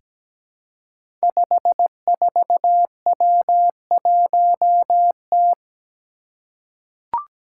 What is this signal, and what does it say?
Morse code, heard as a keyed tone, about 17 wpm, 700 Hz. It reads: 54W1T
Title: Five Challenging Letter-Number Combinations 17wpm